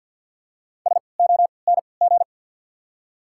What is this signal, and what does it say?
Morse code, heard as a keyed tone, about 50 wpm, spaced as for 28 wpm, 680 Hz. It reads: SONG
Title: 3x Character Spacing - Top 500 Words - 50wpm